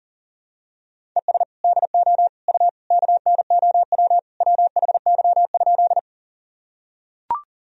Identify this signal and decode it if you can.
Morse code, heard as a keyed tone, about 40 wpm, 700 Hz. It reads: ES DO U KNOW WHY?
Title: Sentences from Top 200 Words and Abbreviations 40wpm